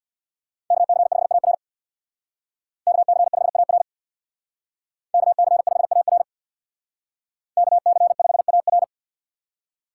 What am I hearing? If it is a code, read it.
Morse code, heard as a keyed tone, about 58 wpm, 700 Hz. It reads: KC5AR KC5AR KC5AR KC5AR
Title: Speed Racing - USA Call Signs 50wpm